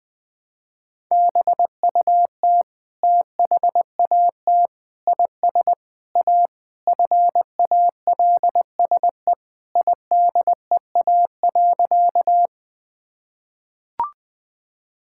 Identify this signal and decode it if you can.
Morse code, heard as a keyed tone, about 20 wpm, 700 Hz. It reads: BUT THAT IS A FALSE IDEA.
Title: Sentences from Top 1600 Words 20wpm